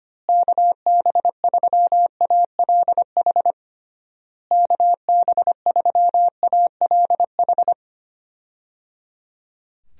Morse code, heard as a keyed tone, about 25 wpm, 700 Hz. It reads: KB3ALH KB3ALH
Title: USA Call Signs - 6 Characters - Repeated 3x - 25wpm